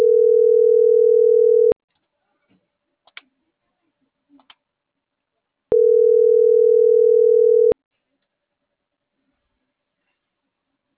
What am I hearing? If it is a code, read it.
Ringback tone (North America), 440 + 480 Hz.
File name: CENTRAL MAYAB / out michelle.martinez